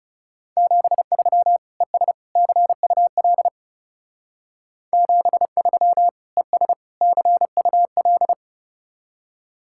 Morse code, heard as a keyed tone, about 32 wpm, 700 Hz. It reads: 73 ES CUL 73 ES CUL
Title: Speed-Racing Common QSO Phrases - 30wpm